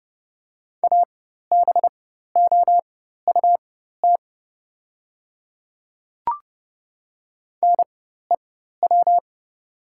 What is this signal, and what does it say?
Morse code, heard as a keyed tone, about 30 wpm, spaced as for 14 wpm, 700 Hz. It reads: ABOUT NEW
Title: 4x Character Spacing - Top 300 Words - 30wpm